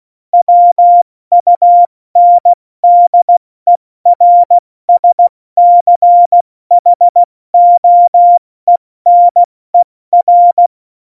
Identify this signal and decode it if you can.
Morse code, heard as a keyed tone, about 16 wpm, 700 Hz. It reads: WUNDERSCHOENER
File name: Morsekurs / Witze / Witz 04. September 2018, 70 F80